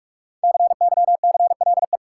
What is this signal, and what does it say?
Morse code, heard as a keyed tone, about 46 wpm, 700 Hz. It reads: CYCLE
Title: Sentences from Top 900 Words - New Word Drill - 45wpm